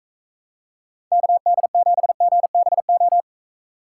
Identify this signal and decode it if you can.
Morse code, heard as a keyed tone, about 45 wpm, 700 Hz. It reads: KD7GBO